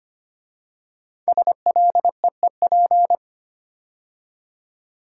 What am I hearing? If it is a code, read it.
Morse code, heard as a keyed tone, about 25 wpm, 700 Hz. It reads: SLEEP
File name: 5 Letter Words - First 3 Letters Given - 25wpm